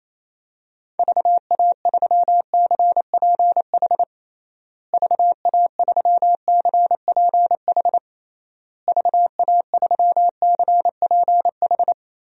Morse code, heard as a keyed tone, about 27 wpm, 700 Hz. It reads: VA3CPH VA3CPH VA3CPH
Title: DXCC-100 Call Signs - 6 Characters - Repeated 3x - 28wpm